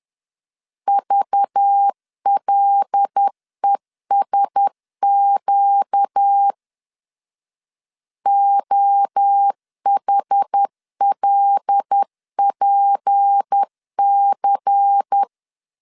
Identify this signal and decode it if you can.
Morse code, heard as a keyed tone, about 10 wpm, 800 Hz. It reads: VLESQ OHLPC
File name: CW / DET / 1 6 1500C S A 10wpm 4141